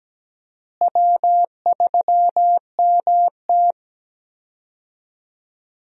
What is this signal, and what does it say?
Morse code, heard as a keyed tone, about 17 wpm, 700 Hz. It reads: W3MT
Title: USA Call Signs - 4 Characters - Repeated 3x - 17wpm